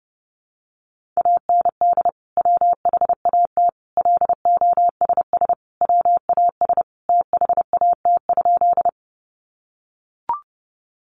Morse code, heard as a keyed tone, about 28 wpm, 700 Hz. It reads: AND WHAT LOSS WAS THAT?
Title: Sentences from Top 1400 Words 30wpm